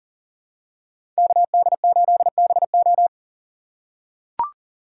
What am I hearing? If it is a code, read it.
Morse code, heard as a keyed tone, about 39 wpm, 700 Hz. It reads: KD8BO
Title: USA Call Signs - 5 Characters - 2x Character Spacing - 40wpm